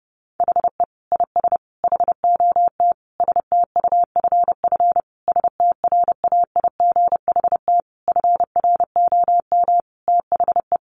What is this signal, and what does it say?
Morse code, heard as a keyed tone, about 30 wpm, 700 Hz. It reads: HE IS HOT STUFF STRAIGHT FROM THE
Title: Sentences from Top 1500 Words 30wpm